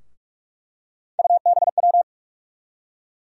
Morse code, heard as a keyed tone, about 47 wpm, 700 Hz. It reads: UBW
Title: USA Call Sign Postfixes - 3 Letters 45wpm